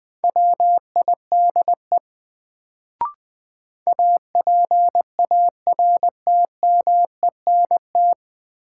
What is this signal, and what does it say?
Morse code, heard as a keyed tone, about 21 wpm, 700 Hz. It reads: WIDE APARTMENT
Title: Sentences from Top 900 Words - New Word Drill - 20wpm